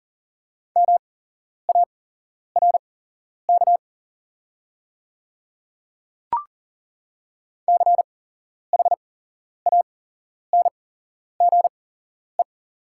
Morse code, heard as a keyed tone, about 40 wpm, spaced as for 11 wpm, 690 Hz. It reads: MARK CHANGE
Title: 8x Character Spacing - Top 300 Words - 40wpm